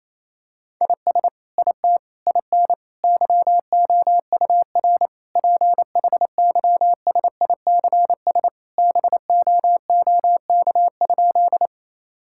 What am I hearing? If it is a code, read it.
Morse code, heard as a keyed tone, about 28 wpm, 700 Hz. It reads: IS IT IN YOUR PHYSICS BOOK?
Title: Sentences from Top 1000 Words - Review Part 5 - 28wpm